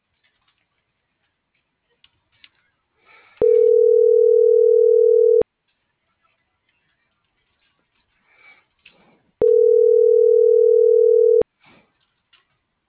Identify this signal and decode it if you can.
Ringback tone (North America), 440 + 480 Hz.
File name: CENTRAL MULTIMODAL / out josimar.cruz - .61249